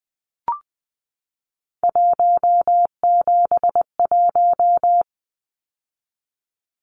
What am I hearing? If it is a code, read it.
Morse code, heard as a keyed tone, about 20 wpm, 700 Hz. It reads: 171
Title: Numbers - 3-Digit 20wpm